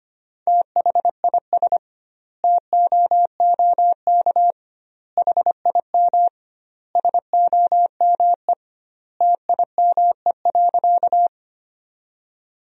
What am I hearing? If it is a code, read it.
Morse code, heard as a keyed tone, about 25 wpm, 700 Hz. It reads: THIS TOOK HIM SOME TIME.
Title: 2x Word-Spacing - Sentences from Top 300 Words - 25wpm